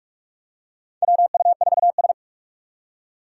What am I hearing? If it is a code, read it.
Morse code, heard as a keyed tone, about 44 wpm, 700 Hz. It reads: WU4S
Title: USA Call Signs - 4 Characters - 2x Character Spacing - 45wpm